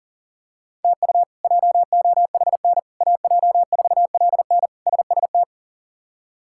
Morse code, heard as a keyed tone, about 37 wpm, 700 Hz. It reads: TU JOHN AJ4LN SST
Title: K1USN SST QSO Practice - 40wpm